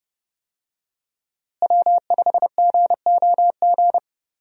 Morse code, heard as a keyed tone, about 30 wpm, 700 Hz. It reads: W5GOG
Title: USA Call Signs - 5 Characters - Repeated 2x - 30wpm